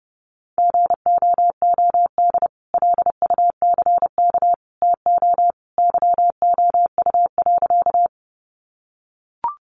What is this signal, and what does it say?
Morse code, heard as a keyed tone, about 30 wpm, 700 Hz. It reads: GOOD LUCK TO YOU.
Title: Sentences from Top 1500 Words 30wpm